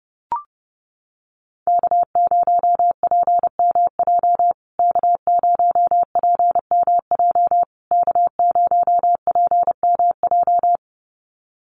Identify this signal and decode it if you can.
Morse code, heard as a keyed tone, about 30 wpm, 700 Hz. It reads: K0PMJ K0PMJ K0PMJ
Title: USA Call Signs Repeated 3x 30wpm